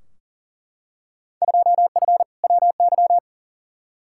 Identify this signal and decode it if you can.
Morse code, heard as a keyed tone, about 40 wpm, 700 Hz. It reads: 2F WY
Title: Field Day Exchanges Encore - 40wpm